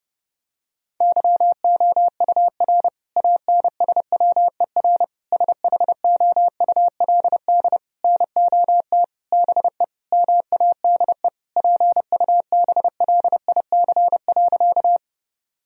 Morse code, heard as a keyed tone, about 30 wpm, 700 Hz. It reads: YOUR ANSWER SHOULD NOT BE MADE PUBLIC.
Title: Sentences from Top 1000 Words - Review Part 3 - 30wpm